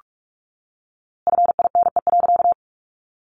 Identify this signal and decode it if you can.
Morse code, heard as a keyed tone, about 44 wpm, 700 Hz. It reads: FINE.
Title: A Sound Of Thunder 45wpm